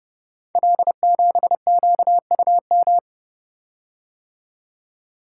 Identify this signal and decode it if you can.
Morse code, heard as a keyed tone, about 29 wpm, 700 Hz. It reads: L7QUM
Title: Code Groups - 5 Letter:Numbers - 30wpm